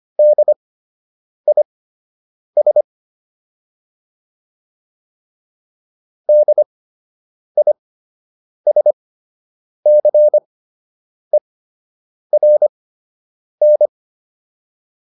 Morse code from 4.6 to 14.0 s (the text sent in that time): DISCERN